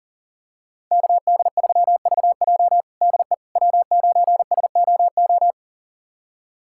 Morse code, heard as a keyed tone, about 40 wpm, 700 Hz. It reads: KD3VJ DE W9SOO
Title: USA Callsign DE Callsign - 5 Characters - 40wpm